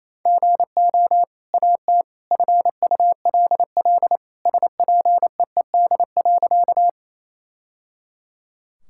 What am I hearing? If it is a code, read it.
Morse code, heard as a keyed tone, about 28 wpm, 700 Hz. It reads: GO AT FULL SPEED.
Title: Sentences from Top 1000 Words - Review Part 5 - 28wpm